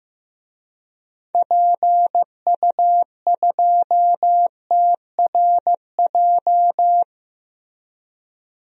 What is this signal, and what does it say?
Morse code, heard as a keyed tone, about 16 wpm, 700 Hz. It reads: PU2TRJ
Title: DXCC-100 Call Signs - 6 Characters - Repeated 3x - 15wpm